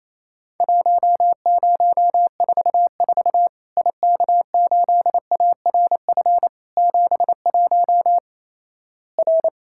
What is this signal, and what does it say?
Morse code, heard as a keyed tone, about 28 wpm, 700 Hz. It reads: 1044 IK8ARF 71 R
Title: Worked All Europe DX Contest - QTC Exchange - 28wpm